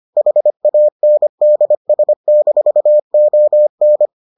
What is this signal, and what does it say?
Morse code, heard as a keyed tone, about 25 wpm, 600 Hz. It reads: HANDS-ON